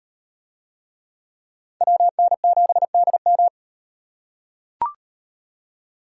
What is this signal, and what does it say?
Morse code, heard as a keyed tone, about 37 wpm, 700 Hz. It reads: WN7DM